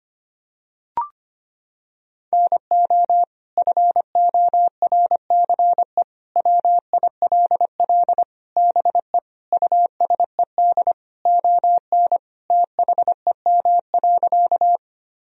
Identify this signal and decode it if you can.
Morse code, heard as a keyed tone, about 25 wpm, 700 Hz. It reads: NO FORCE WILL BE USED ON THEM.
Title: Sentences from Top 1000 Words - Review Part 3 - 25wpm